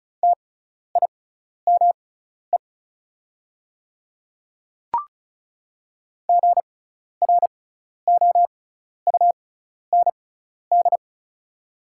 Morse code, heard as a keyed tone, about 36 wpm, spaced as for 12 wpm, 700 Hz. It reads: TIME GROUND
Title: 6x Character Spacing - Top 500 Words - 35wpm